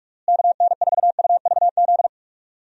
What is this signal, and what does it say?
Morse code, heard as a keyed tone, about 46 wpm, 700 Hz. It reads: KN4UVZ
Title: USA Call Signs - 6 Characters - Repeated 2x - 45wpm